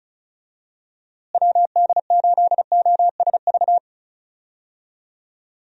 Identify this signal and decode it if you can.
Morse code, heard as a keyed tone, about 36 wpm, 700 Hz. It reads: WD8OSV
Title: USA Call Signs - 6 Characters - Repeated 3x - 35wpm